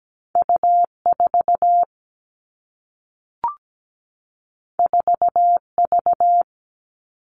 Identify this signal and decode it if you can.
Morse code, heard as a keyed tone, about 17 wpm, 700 Hz. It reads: U4 4V
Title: All Cut Numbers - 2-Digit 17wpm